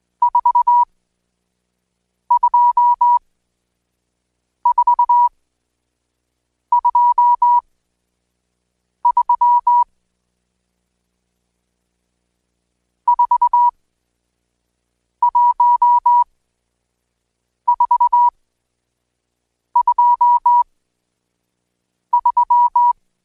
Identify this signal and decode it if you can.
Morse code, heard as a keyed tone, about 20 wpm, spaced as for 5 wpm, 970 Hz. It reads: 42423 41423
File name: Code Quick / AudioDownld1 / Seg76-Day8-DayDrill 8